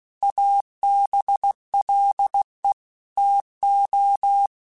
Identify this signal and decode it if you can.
Morse code, heard as a keyed tone, about 16 wpm, 800 Hz. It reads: ABLE TO